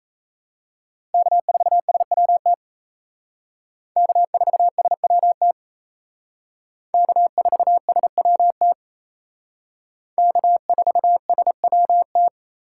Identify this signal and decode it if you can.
Morse code, heard as a keyed tone, about 35 wpm, 700 Hz. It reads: K4SWT K4SWT K4SWT K4SWT